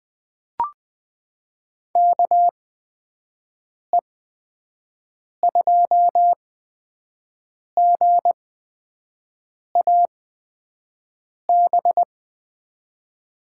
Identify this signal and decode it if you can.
Morse code, heard as a keyed tone, about 21 wpm, spaced as for 6 wpm, 700 Hz. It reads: KE2GAB
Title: USA Call Signs - 6 Characters - 8x Character Spacing - 20wpm